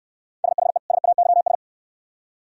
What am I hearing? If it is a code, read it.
Morse code, heard as a keyed tone, about 68 wpm, 700 Hz. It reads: SHE SAYS